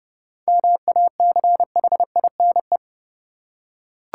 Morse code, heard as a keyed tone, about 32 wpm, 700 Hz. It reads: MACHINE
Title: Seven Letter Words - 30wpm